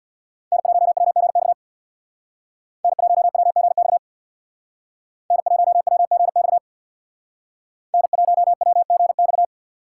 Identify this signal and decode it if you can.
Morse code, heard as a keyed tone, about 64 wpm, 700 Hz. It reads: N1WGX N1WGX N1WGX N1WGX